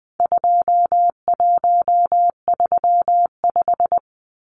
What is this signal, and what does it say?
Morse code, heard as a keyed tone, about 20 wpm, 700 Hz. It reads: 2135
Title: Numbers - 4-Digit 20wpm